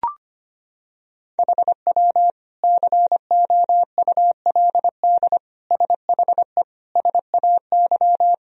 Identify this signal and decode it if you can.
Morse code, heard as a keyed tone, about 25 wpm, 700 Hz. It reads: HW COULD SHE SAY
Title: Sentences from Top 200 Words and Abbreviations 25wpm